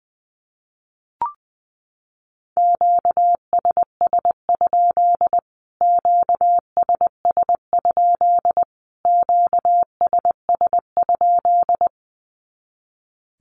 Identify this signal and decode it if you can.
Morse code, heard as a keyed tone, about 20 wpm, 700 Hz. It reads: QSS? QSS? QSS?
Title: Q-Codes Repeated 3x 20wpm